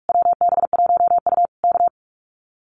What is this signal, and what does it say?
Morse code, heard as a keyed tone, about 46 wpm, 700 Hz. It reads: WB1U K
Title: Common QSO Phrases Encore 45wpm